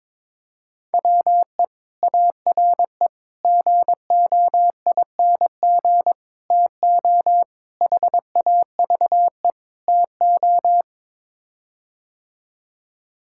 Morse code, heard as a keyed tone, about 23 wpm, 700 Hz. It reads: WE ARE GOING TO HAVE TO
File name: practice / 22wpm / Sets of 6 Words - 22wpm